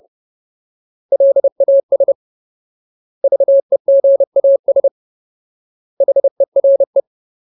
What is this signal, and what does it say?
Morse code, heard as a keyed tone, about 30 wpm, 550 Hz. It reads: LAS VEGAS HERE
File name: Ultimate Word List / 30WPM Ultimate